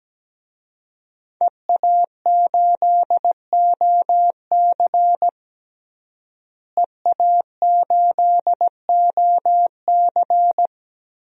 Morse code, heard as a keyed tone, about 17 wpm, 700 Hz. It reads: EA8OC EA8OC
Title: DXCC-100 Call Signs - 5 Characters - Repeated 3x - 17wpm